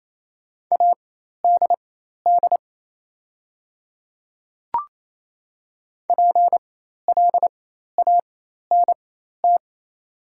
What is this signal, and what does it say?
Morse code, heard as a keyed tone, about 27 wpm, spaced as for 13 wpm, 700 Hz. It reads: ADD PLANT
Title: 4x Character Spacing - Top 300 Words - 28wpm